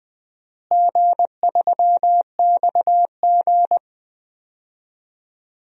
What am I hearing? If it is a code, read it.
Morse code, heard as a keyed tone, about 20 wpm, 700 Hz. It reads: G3XG
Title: DXCC-100 Call Signs - 4 Characters - Repeated 3x - 20wpm